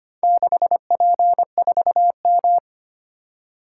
Morse code, heard as a keyed tone, about 25 wpm, 700 Hz. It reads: 6P4M
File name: practice / 25wpm / Four Letter-Number Combinations - 25wpm